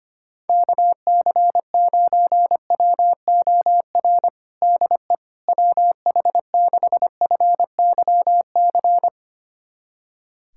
Morse code, heard as a keyed tone, about 25 wpm, 700 Hz. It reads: KC9WOR DE WH6FYC
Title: USA Callsign DE Callsign 25wpm